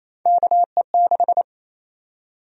Morse code, heard as a keyed tone, about 28 wpm, 700 Hz. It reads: KE6
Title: USA Call Sign Prefixes - Repeated 2x - 28wpm